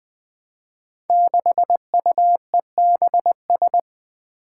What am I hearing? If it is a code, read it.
Morse code, heard as a keyed tone, about 20 wpm, 700 Hz. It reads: 6UEBS
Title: Five Challenging Letter-Number Combinations 20wpm